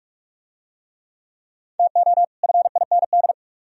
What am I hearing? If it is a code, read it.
Morse code, heard as a keyed tone, about 44 wpm, 700 Hz. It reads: TO FIND